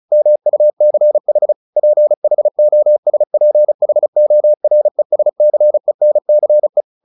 Morse code, heard as a keyed tone, about 34 wpm, 600 Hz. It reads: MUCH PHOSPHORESCENCE
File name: the old man and the sea / the old man and the sea 1 35wpm